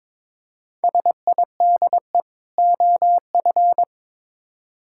Morse code, heard as a keyed tone, about 22 wpm, 700 Hz. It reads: SIDE OF
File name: Build A Phrase - 2-Word Phrases - 22wpm